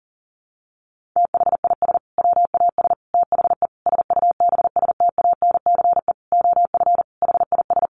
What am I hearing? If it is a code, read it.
Morse code, heard as a keyed tone, about 40 wpm, 700 Hz. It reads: THIS WAS THE SUBSTANCE OF HIS
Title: Sentences from Top 1500 Words 40wpm